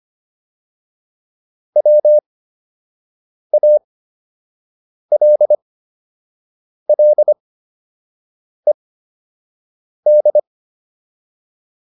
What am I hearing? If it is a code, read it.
Morse code, heard as a keyed tone, about 25 wpm, spaced as for 6 wpm, 600 Hz. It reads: WALLED